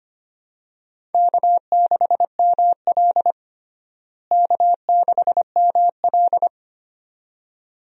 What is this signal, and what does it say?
Morse code, heard as a keyed tone, about 25 wpm, 700 Hz. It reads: K6ML K6ML
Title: USA Call Signs - 4 Characters - Repeated 2x - 25wpm